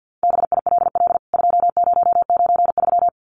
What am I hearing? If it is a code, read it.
Morse code, heard as a keyed tone, about 48 wpm, 700 Hz. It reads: BILL 2093
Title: CWT Exchange 50wpm